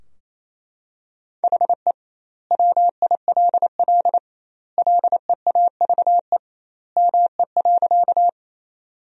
Morse code, heard as a keyed tone, about 28 wpm, 700 Hz. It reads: HE WILL LEAVE ME.
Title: 2x Word-Spacing - Sentences from Top 400 Words - 28wpm